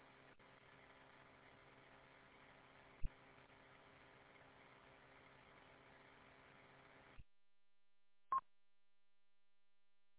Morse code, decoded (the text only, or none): none